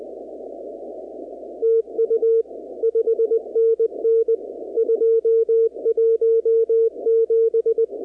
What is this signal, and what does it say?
Morse code, heard as a keyed tone, about 20 wpm, 450 Hz. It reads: TU 5NN 217
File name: contests / 2019-cw-wpx-kt5j-k5pi- -2330 0-44